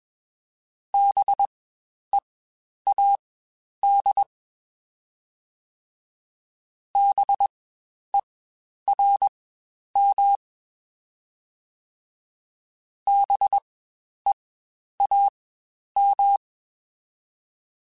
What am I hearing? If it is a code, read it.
Morse code, heard as a keyed tone, about 21 wpm, spaced as for 10 wpm, 800 Hz. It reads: BEAD BERM BEAM
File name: Spaced Alphas / Spaced Alphas mp3s / Bs sp @20